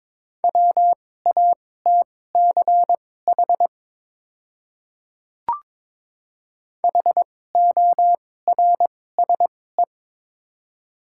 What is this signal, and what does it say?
Morse code, heard as a keyed tone, about 22 wpm, spaced as for 16 wpm, 700 Hz. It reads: WATCH HORSE
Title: 2x Character Spacing - Top 300 Words - 22wpm